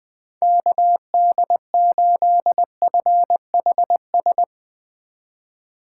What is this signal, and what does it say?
Morse code, heard as a keyed tone, about 21 wpm, 700 Hz. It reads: KD8FHS